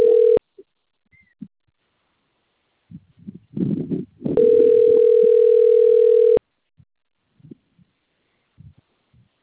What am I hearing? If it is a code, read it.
Ringback tone (North America), 440 + 480 Hz.
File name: out- - - - - .99791